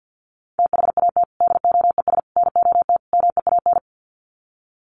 Morse code, heard as a keyed tone, about 50 wpm, 700 Hz. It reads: THAT DOES NOT MEAN
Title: Sets of 4 Words 50wpm